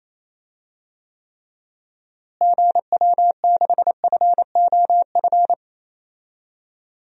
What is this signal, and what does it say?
Morse code, heard as a keyed tone, about 27 wpm, 700 Hz. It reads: GW6FOF
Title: DXCC-100 Call Signs - 6 Characters - Repeated 2x - 28wpm